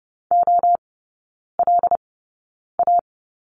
Morse code, heard as a keyed tone, about 30 wpm, spaced as for 9 wpm, 700 Hz. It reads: OLA
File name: Lesson 17e - Letter u - Three Characters 30wpm